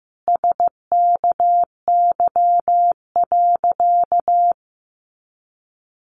Morse code, heard as a keyed tone, about 15 wpm, 700 Hz. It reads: SKY.
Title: A Sound Of Thunder 15wpm